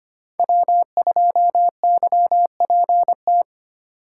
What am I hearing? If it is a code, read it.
Morse code, heard as a keyed tone, about 25 wpm, 700 Hz. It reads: W2YPT
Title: USA Call Signs - 5 Characters - 2x Character Spacing - 25wpm